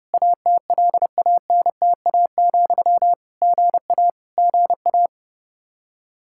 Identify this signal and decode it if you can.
Morse code, heard as a keyed tone, about 30 wpm, 700 Hz. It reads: ATLANTA, GA GA
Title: K1USN SST QSO Practice 30wpm